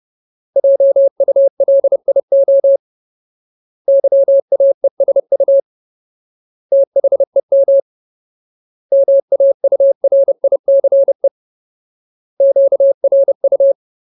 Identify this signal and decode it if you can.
Morse code, heard as a keyed tone, about 30 wpm, 550 Hz. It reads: JULIO YAESU THEM MAURICE QRU
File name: Ultimate Word List / 30WPM Ultimate